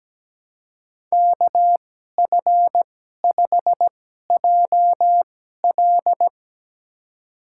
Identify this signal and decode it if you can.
Morse code, heard as a keyed tone, about 17 wpm, spaced as for 12 wpm, 700 Hz. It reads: KF5JL
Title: USA Call Signs - 5 Characters - 2x Character Spacing - 17wpm